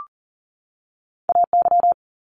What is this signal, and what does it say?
Morse code, heard as a keyed tone, about 41 wpm, 700 Hz. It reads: AY
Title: Multi-letter Phonemes 40wpm